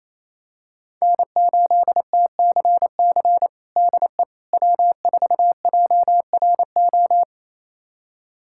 Morse code, heard as a keyed tone, about 28 wpm, 700 Hz. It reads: N8TCC DE W4JRO
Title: USA Callsign DE Callsign - 5 Characters - 28wpm